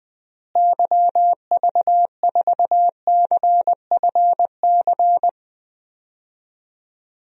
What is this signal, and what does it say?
Morse code, heard as a keyed tone, about 20 wpm, 700 Hz. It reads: YV4CFC
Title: DXCC-100 Call Signs - 6 Characters - Repeated 2x - 20wpm